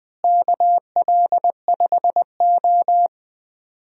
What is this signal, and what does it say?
Morse code, heard as a keyed tone, about 19 wpm, 700 Hz. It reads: KL5O
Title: USA Call Signs - 4 Characters - Repeated 3x - 20wpm